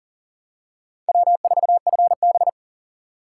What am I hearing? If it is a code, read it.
Morse code, heard as a keyed tone, about 40 wpm, 700 Hz. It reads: W4FB
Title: USA Call Signs - 4 Characters - 3x Character Spacing - 40wpm